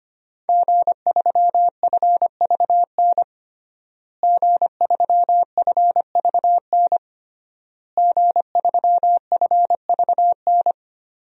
Morse code, heard as a keyed tone, about 25 wpm, 700 Hz. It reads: G3FVN G3FVN G3FVN
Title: DXCC-100 Call Signs - 5 Characters - Repeated 3x - 25wpm